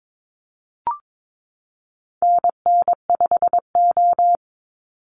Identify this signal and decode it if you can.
Morse code, heard as a keyed tone, about 22 wpm, 700 Hz. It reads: NN5O
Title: Call Signs Repeated 3x - 22wpm